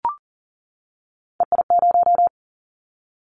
Morse code, heard as a keyed tone, about 39 wpm, 700 Hz. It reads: EI0
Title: Callsign Prefixes for Top 50 DXCC - 40wpm